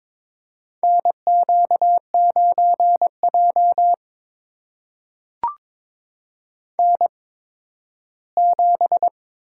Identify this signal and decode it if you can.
Morse code, heard as a keyed tone, about 22 wpm, spaced as for 13 wpm, 700 Hz. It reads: NQ9J N7
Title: USA Call Signs - 8x Character Spacing - 22wpm